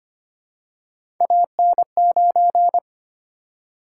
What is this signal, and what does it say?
Morse code, heard as a keyed tone, about 25 wpm, 700 Hz. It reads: AN9
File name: Three Letter-Number Combinations - 25wpm